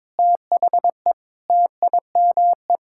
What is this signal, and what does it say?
Morse code, heard as a keyed tone, about 22 wpm, 700 Hz. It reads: THE TIME
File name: Sentences with 7-Word Phrases - 22wpm